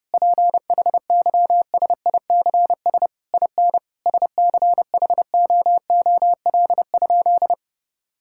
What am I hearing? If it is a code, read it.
Morse code, heard as a keyed tone, about 30 wpm, 700 Hz. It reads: PHYSICS IN SCHOOL?
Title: Sentences from Top 1000 Words - Review Part 5 - 30wpm